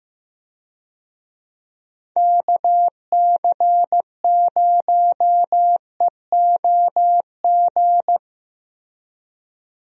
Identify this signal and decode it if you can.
Morse code, heard as a keyed tone, about 15 wpm, 700 Hz. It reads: KC0EOG
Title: USA Call Signs - 6 Characters - Repeated 3x - 15wpm